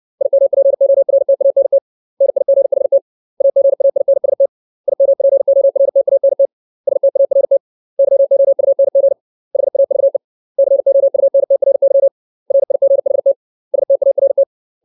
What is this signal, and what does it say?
Morse code, heard as a keyed tone, about 60 wpm, 550 Hz. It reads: IMPORTANT NIGHT AGAINST IMPORTANT START YOUNG HAVE COUNTRY RIGHT START
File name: 632 words / 632 Words 60WPM